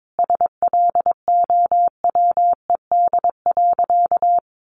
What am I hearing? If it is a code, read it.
Morse code, heard as a keyed tone, about 22 wpm, 700 Hz. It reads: SLOWED.
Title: A Sound Of Thunder 22wpm